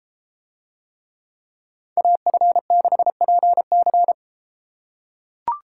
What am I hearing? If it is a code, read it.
Morse code, heard as a keyed tone, about 30 wpm, 700 Hz. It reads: AF6PC